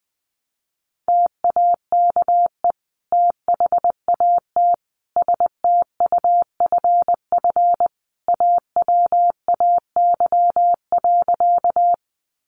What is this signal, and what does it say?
Morse code, heard as a keyed tone, about 19 wpm, 700 Hz. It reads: TAKE THAT STUFF AWAY.
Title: Sentences from Top 1400 Words 20wpm